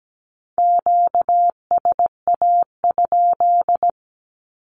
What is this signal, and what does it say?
Morse code, heard as a keyed tone, about 17 wpm, 700 Hz. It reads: QSA?
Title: Q-Codes 17wpm